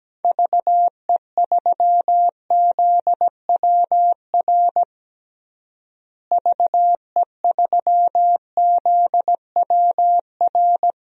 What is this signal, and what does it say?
Morse code, heard as a keyed tone, about 17 wpm, 700 Hz. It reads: VE3ZWR VE3ZWR
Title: DXCC-100 Call Signs - 6 Characters - Repeated 3x - 17wpm